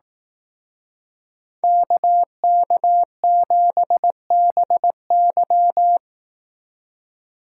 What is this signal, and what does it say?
Morse code, heard as a keyed tone, about 18 wpm, 700 Hz. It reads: KK7BY
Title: Speed Racing - USA 2x2 Call Signs 15wpm